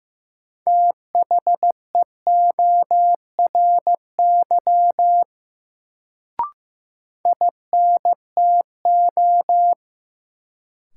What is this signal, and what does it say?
Morse code, heard as a keyed tone, about 15 wpm, 700 Hz. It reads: THEORY INTO
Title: Sentences from Top 600 Words - New Word Drill - 15wpm